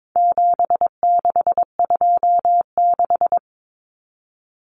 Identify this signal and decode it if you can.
Morse code, heard as a keyed tone, about 22 wpm, 700 Hz. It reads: 7626
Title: Numbers - 4-Digit 22wpm